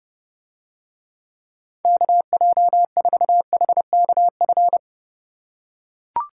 Morse code, heard as a keyed tone, about 31 wpm, 700 Hz. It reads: KJ4HKF